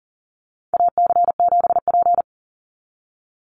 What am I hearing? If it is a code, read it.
Morse code, heard as a keyed tone, about 40 wpm, 700 Hz. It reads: AC7P
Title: NA SOTA Chaser Callsigns 40wpm